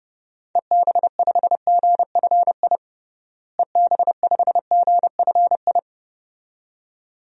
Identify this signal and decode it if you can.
Morse code, heard as a keyed tone, about 30 wpm, 700 Hz. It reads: EB5GFI EB5GFI
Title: DXCC-100 Call Signs - 6 Characters - Repeated 2x - 30wpm